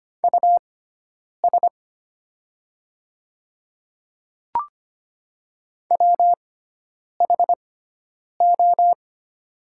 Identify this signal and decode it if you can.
Morse code, heard as a keyed tone, about 25 wpm, spaced as for 9 wpm, 700 Hz. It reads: US WHO
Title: 6x Character Spacing - Top 300 Words - 25wpm